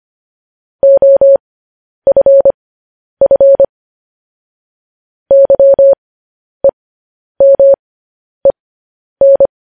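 Morse code, heard as a keyed tone, about 25 wpm, spaced as for 10 wpm, 550 Hz. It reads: OFF YEMEN
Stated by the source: Morse code news headlines sent at 10.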